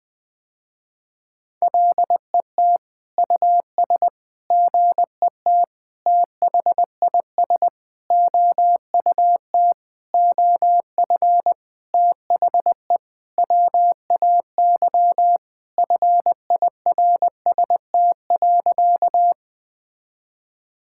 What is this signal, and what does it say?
Morse code, heard as a keyed tone, about 20 wpm, 700 Hz. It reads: LET US GET THIS OUT OF THE WAY FIRST.
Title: Sentences with 7-Word Phrases - Spoken Beginning - 20wpm